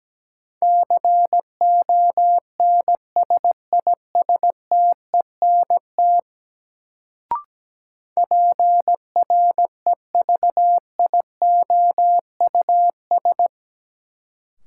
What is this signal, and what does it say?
Morse code, heard as a keyed tone, about 17 wpm, 700 Hz. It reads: CONSISTENT PREVIOUS
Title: Sentences from Top 1500 Words - New Word Drill - 17wpm